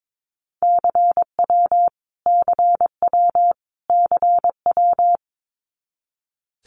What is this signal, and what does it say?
Morse code, heard as a keyed tone, about 22 wpm, 700 Hz. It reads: CW CW CW
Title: Abbreviations Repeated 3x - 22wpm